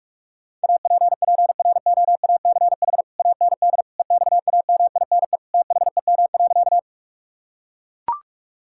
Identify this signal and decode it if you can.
Morse code, heard as a keyed tone, about 41 wpm, 700 Hz. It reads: APPROACH AND EXAMINE THEM.
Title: Sentences from Top 1600 Words 45wpm